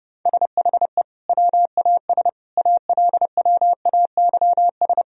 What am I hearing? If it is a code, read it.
Morse code, heard as a keyed tone, about 30 wpm, 700 Hz. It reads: SHE WAS ALWAYS
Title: Sentences from Top 1000 Words - Review Part 3 - 30wpm